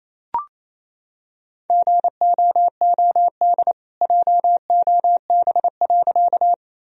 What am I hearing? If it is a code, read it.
Morse code, heard as a keyed tone, about 28 wpm, 700 Hz. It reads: GOOD JOB.
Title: Sentences from Top 1000 Words - Review Part 3 - 28wpm